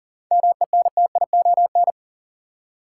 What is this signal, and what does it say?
Morse code, heard as a keyed tone, about 40 wpm, 700 Hz. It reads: MENTION
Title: Seven Letter Words - 40wpm